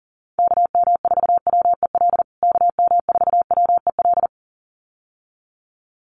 Morse code, heard as a keyed tone, about 40 wpm, 700 Hz. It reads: KM4WEL KM4WEL
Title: USA Call Signs Repeated 2x 40wpm